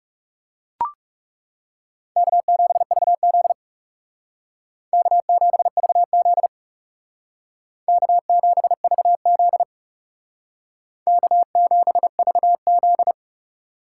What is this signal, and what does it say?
Morse code, heard as a keyed tone, about 38 wpm, 700 Hz. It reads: K7VZ K7VZ K7VZ K7VZ